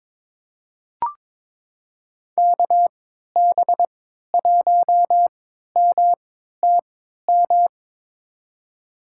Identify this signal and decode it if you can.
Morse code, heard as a keyed tone, about 22 wpm, spaced as for 12 wpm, 700 Hz. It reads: KB1MTM
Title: USA Call Signs - 6 Characters - 3x Character Spacing - 22wpm